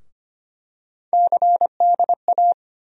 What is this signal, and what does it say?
Morse code, heard as a keyed tone, about 25 wpm, 700 Hz. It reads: CDA
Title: USA Call Sign Postfixes - 3 Letters 25wpm